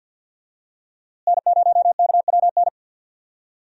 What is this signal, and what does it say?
Morse code, heard as a keyed tone, about 47 wpm, 700 Hz. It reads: N0KWN
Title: USA Call Signs - 5 Characters - Repeated 2x - 50wpm